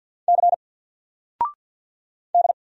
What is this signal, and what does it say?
Morse code, heard as a keyed tone, about 49 wpm, 700 Hz. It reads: C D
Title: Single Letter-Number 50wpm